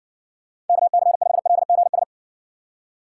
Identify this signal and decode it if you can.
Morse code, heard as a keyed tone, about 60 wpm, 700 Hz. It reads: KC5LGS